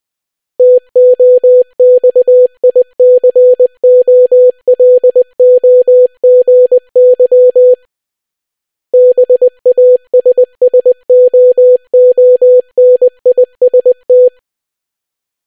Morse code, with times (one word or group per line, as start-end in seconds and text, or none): 0.6-7.7 TOXICOLOGY
8.9-14.3 BASSOONIST